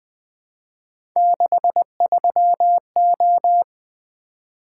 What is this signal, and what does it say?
Morse code, heard as a keyed tone, about 20 wpm, 700 Hz. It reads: 63O
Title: Three Letter-Number Combinations - 20wpm